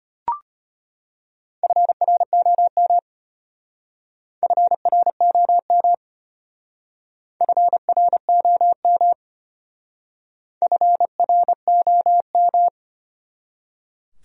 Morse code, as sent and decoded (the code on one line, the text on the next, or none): ..-. .-. --- -- / ..-. .-. --- -- / ..-. .-. --- -- / ..-. .-. --- --
FROM FROM FROM FROM